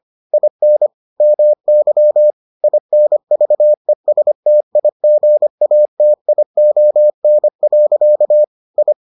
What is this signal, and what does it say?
Morse code, heard as a keyed tone, about 25 wpm, 600 Hz. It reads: IN MY INVESTIGATION. I